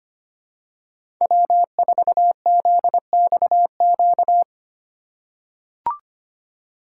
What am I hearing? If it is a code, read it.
Morse code, heard as a keyed tone, about 25 wpm, 700 Hz. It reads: W4ZXQ
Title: USA Call Signs - 5 Characters - 8x Character Spacing - 25wpm